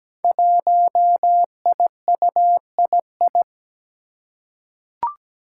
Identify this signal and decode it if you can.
Morse code, heard as a keyed tone, about 17 wpm, 700 Hz. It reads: 1IUII
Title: Five Challenging Letter-Number Combinations 17wpm